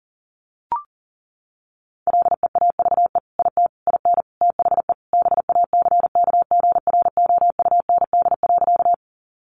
Morse code, heard as a keyed tone, about 40 wpm, 700 Hz. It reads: LEAVE IT IN THE BACKGROUND.
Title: Sentences from Top 1300 Words 40wpm